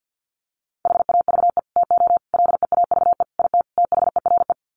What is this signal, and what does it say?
Morse code, heard as a keyed tone, about 50 wpm, 700 Hz. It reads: HAVE TO LEAVE IT THERE
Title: Sets of 5 Words Encore2 50wpm